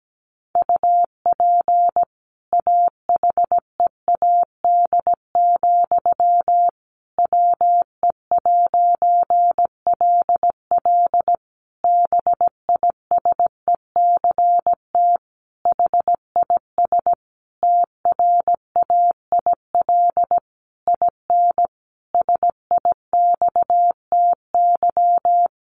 Morse code, heard as a keyed tone, about 17 wpm, 700 Hz. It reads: UP AHEAD, WE'LL BISECT HIS TRAIL IN SIXTY
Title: A Sound Of Thunder 17wpm